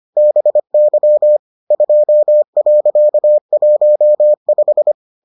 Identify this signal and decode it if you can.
Morse code, heard as a keyed tone, about 25 wpm, 600 Hz. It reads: BY 2.15